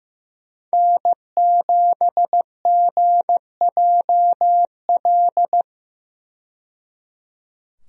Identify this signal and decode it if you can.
Morse code, heard as a keyed tone, about 16 wpm, 700 Hz. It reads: N7GJL
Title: USA Call Signs - 5 Characters - 15wpm